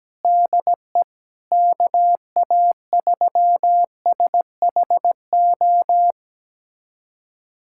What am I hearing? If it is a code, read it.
Morse code, heard as a keyed tone, about 17 wpm, 700 Hz. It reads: DE KA3SHO
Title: USA Callsign DE Callsign 17wpm